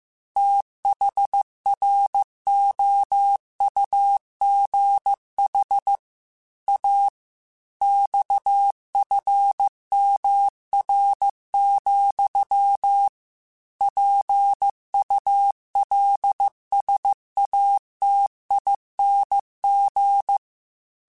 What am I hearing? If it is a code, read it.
Morse code, heard as a keyed tone, about 15 wpm, 800 Hz. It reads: THROUGH A XFMR, PULSATING
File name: Technical Subjects / Technical Subjects-mp3s / Sw Pwr Sup @13